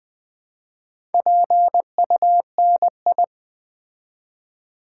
Morse code, heard as a keyed tone, about 20 wpm, 700 Hz. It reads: PUNI
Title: Build A Word - 2 Letters At A Time - 6-Letter Words - 20wpm